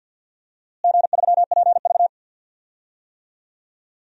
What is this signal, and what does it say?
Morse code, heard as a keyed tone, about 52 wpm, 700 Hz. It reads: G3PV
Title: DXCC-100 Call Signs - 4 Characters - Repeated 3x - 50wpm